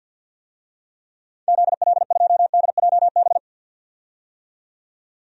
Morse code, heard as a keyed tone, about 47 wpm, 700 Hz. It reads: ZL2DJB
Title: DXCC-100 Call Signs - 6 Characters - 50wpm